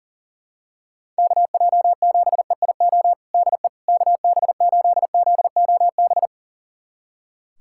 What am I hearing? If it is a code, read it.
Morse code, heard as a keyed tone, about 40 wpm, 700 Hz. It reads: KJ7EIO DE KB8ZOB
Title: USA Callsign DE Callsign 40wpm